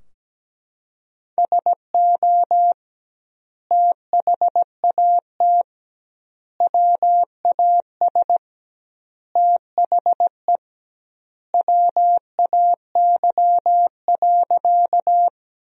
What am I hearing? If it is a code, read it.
Morse code, heard as a keyed tone, about 17 wpm, 700 Hz. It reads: SO THAT WAS THE WAY.
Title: 2x Word-Spacing - Sentences from Top 100 Words - 17wpm